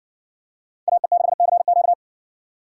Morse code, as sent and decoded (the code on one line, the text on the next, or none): .- . -.... -.-. --.-
AE6CQ